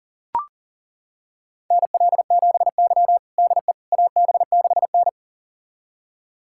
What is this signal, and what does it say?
Morse code, heard as a keyed tone, about 40 wpm, 700 Hz. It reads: NL7Y DE AB6N
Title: USA Callsign DE Callsign - 4 Characters - 40wpm